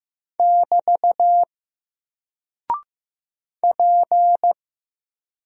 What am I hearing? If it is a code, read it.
Morse code, heard as a keyed tone, about 15 wpm, 700 Hz. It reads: <BT> P
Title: Single Character (Letter:Number:Punctuation:Prosign) Alt 15wpm